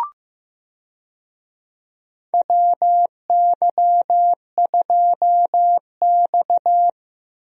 Morse code, heard as a keyed tone, about 15 wpm, 700 Hz. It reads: WY2X
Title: USA Call Signs - 4 Characters - Repeated 2x - 15wpm